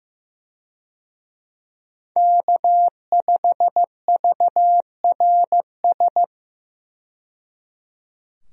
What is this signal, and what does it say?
Morse code, heard as a keyed tone, about 15 wpm, 700 Hz. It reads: K5VRS
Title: USA Call Signs - 5 Characters - 15wpm